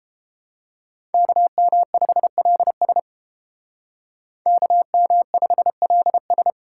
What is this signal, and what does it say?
Morse code, heard as a keyed tone, about 30 wpm, 700 Hz. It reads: KM5LS KM5LS